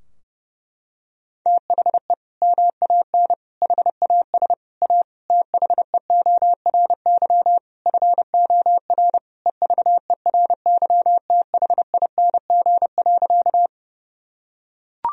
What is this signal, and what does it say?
Morse code, heard as a keyed tone, about 30 wpm, 700 Hz. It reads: THE MAN HAS A THEORY FOR EVERYTHING.
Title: Sentences from Top 1000 Words - Review Part 3 - 30wpm